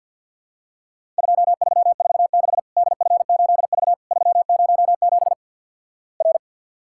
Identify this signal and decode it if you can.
Morse code, heard as a keyed tone, about 49 wpm, 700 Hz. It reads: 2346 DF8V 307 R
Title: Worked All Europe DX Contest - QTC Exchange - 50wpm